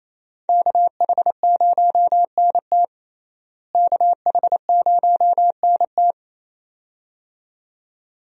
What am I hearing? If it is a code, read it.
Morse code, heard as a keyed tone, about 28 wpm, 700 Hz. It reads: KH0NT KH0NT
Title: USA Call Signs - 5 Characters - Repeated 2x - 28wpm